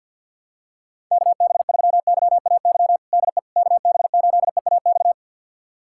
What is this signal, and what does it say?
Morse code, heard as a keyed tone, about 46 wpm, 700 Hz. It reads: KB3YAY DE KB8EAX
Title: USA Callsign DE Callsign - 6 Characters - 50wpm